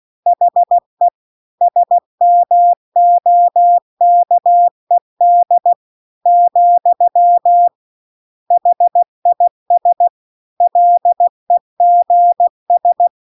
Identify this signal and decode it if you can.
Morse code, heard as a keyed tone, about 16 wpm, 700 Hz. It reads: HE SMOKED , HIS LEGS